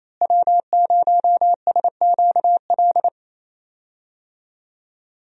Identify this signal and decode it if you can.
Morse code, heard as a keyed tone, about 27 wpm, 700 Hz. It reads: W0SQL
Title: USA Call Signs - 5 Characters - Repeated 2x - 28wpm